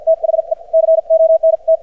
Morse code, heard as a keyed tone, about 50 wpm, spaced as for 43 wpm, 590 Hz. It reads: THE CONT
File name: archived / CRSnet